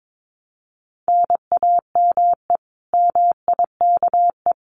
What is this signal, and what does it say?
Morse code, heard as a keyed tone, about 22 wpm, 700 Hz. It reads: NAME MIKE
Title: Common QSO Phrases Encore 22wpm